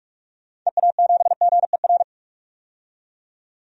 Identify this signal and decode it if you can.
Morse code, heard as a keyed tone, about 42 wpm, 700 Hz. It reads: EA7GER
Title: DXCC-100 Call Signs - 6 Characters - 45wpm